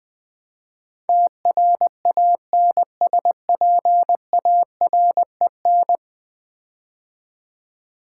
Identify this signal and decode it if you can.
Morse code, heard as a keyed tone, about 20 wpm, 700 Hz. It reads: TRANSPAREN
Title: Build A Word - 5 Letters At A Time - 12-Letter Words - 20wpm